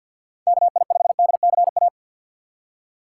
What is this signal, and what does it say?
Morse code, heard as a keyed tone, about 53 wpm, 700 Hz. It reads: KI5DCA
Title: USA Call Signs - 6 Characters - 8x Character Spacing - 50wpm